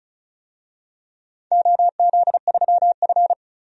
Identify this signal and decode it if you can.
Morse code, heard as a keyed tone, about 34 wpm, 700 Hz. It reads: OZ3F
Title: DXCC-100 Call Signs - 4 Characters - 35wpm